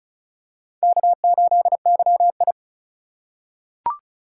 Morse code, heard as a keyed tone, about 36 wpm, 700 Hz. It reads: K8YI
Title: USA Call Signs - 4 Characters - 3x Character Spacing - 35wpm